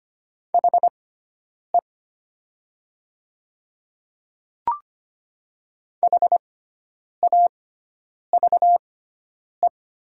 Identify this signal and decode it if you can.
Morse code, heard as a keyed tone, about 25 wpm, spaced as for 9 wpm, 700 Hz. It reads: HE HAVE
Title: 6x Character Spacing - Top 100 Words - 25wpm